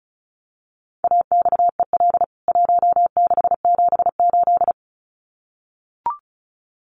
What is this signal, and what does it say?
Morse code, heard as a keyed tone, about 34 wpm, 700 Hz. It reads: AXEL 1678
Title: CWT Exchange 35wpm